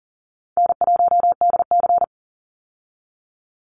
Morse code, heard as a keyed tone, about 40 wpm, 700 Hz. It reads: N1DC
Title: Call Signs - 40wpm